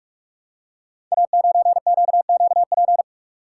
Morse code, heard as a keyed tone, about 44 wpm, 700 Hz. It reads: A9QQP
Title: Code Groups - 5 Letter:Numbers - 45wpm